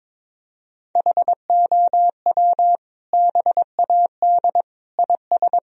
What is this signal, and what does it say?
Morse code, heard as a keyed tone, about 22 wpm, 700 Hz. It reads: HOW BAD IS
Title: Sentences from Top 1000 Words - Review Part 3 - 22wpm